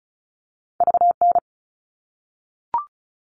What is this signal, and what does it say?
Morse code, heard as a keyed tone, about 36 wpm, 700 Hz. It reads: VN
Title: All Cut Numbers - 2-Digit 35wpm